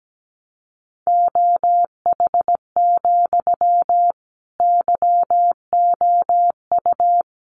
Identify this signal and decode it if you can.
Morse code, heard as a keyed tone, about 17 wpm, 700 Hz. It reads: OH, YOU
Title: Sentences from Top 1000 Words 17wpm